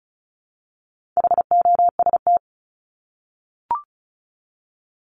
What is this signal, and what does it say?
Morse code, heard as a keyed tone, about 34 wpm, 700 Hz. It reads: HOST
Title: Top 1501-1750 Words 35wpm